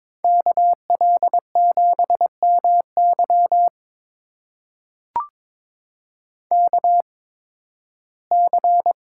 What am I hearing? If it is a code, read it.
Morse code, heard as a keyed tone, about 23 wpm, spaced as for 15 wpm, 700 Hz. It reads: KL7MY KC
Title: USA Call Signs - 5 Characters - 8x Character Spacing - 22wpm